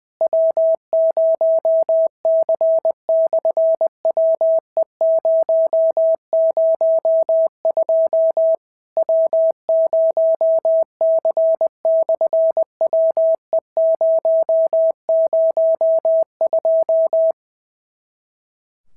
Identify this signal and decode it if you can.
Morse code, heard as a keyed tone, about 19 wpm, 650 Hz. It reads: W0C/WE002 W0C/WE002
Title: SOTA QSO Practice 20wpm